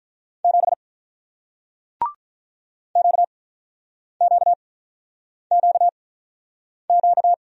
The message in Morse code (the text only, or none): 7 QQQQ